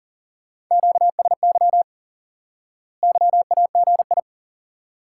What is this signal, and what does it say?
Morse code, heard as a keyed tone, about 37 wpm, 700 Hz. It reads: QSY YAGI
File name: IQE Course 3 - Lesson 89 - YAGI - Two Elements - 40wpm